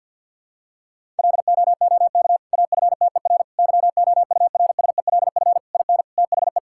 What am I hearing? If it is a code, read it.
Morse code, heard as a keyed tone, about 50 wpm, 700 Hz. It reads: LOOK AFTER YOURSELF IN THE
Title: Sentences from Top 1000 Words - Review Part 6 - 50wpm